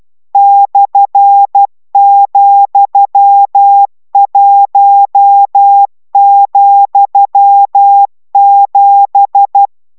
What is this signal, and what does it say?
Morse code, heard as a keyed tone, about 12 wpm, 800 Hz. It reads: /,1,7